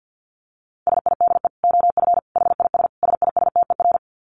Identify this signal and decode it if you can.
Morse code, heard as a keyed tone, about 50 wpm, 700 Hz. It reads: SIDE OF HIS SISTER
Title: Dit Challenge 50wpm